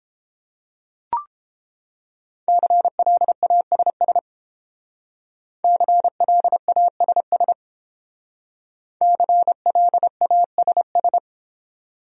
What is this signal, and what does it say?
Morse code, heard as a keyed tone, about 30 wpm, 700 Hz. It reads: CLASS CLASS CLASS